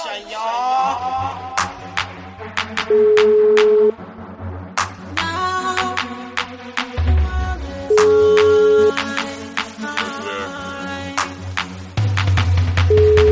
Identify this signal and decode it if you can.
Ringback tone (Europe, 425 Hz), with speech.